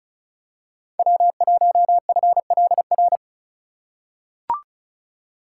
Morse code, heard as a keyed tone, about 36 wpm, 690 Hz. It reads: W1FLR
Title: USA Call Signs - 5 Characters - 3x Character Spacing - 35wpm